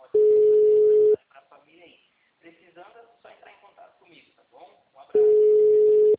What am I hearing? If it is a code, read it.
Ringback tone (Europe, 425 Hz), with speech.
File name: out- -2003- - - .10442